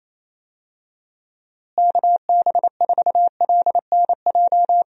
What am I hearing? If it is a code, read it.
Morse code, heard as a keyed tone, about 28 wpm, 700 Hz. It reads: KB4LNJ
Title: USA Call Signs - 6 Characters - Repeated 3x - 28wpm